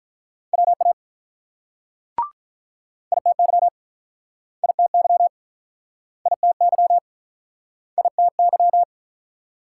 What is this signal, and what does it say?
Morse code, heard as a keyed tone, about 49 wpm, 700 Hz. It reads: WA ITY ITY ITY ITY